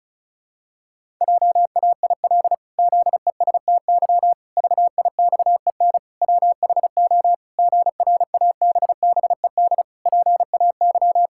Morse code, heard as a keyed tone, about 36 wpm, 700 Hz. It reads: JAIL ZESTY VIXEN WHO GRABBED PAY
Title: Pangram Challenge v2 - 35wpm